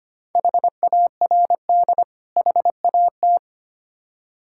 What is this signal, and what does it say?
Morse code, heard as a keyed tone, about 25 wpm, 690 Hz. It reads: HARD HAT
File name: practice / 25wpm / Difficult Words with Hints - 25wpm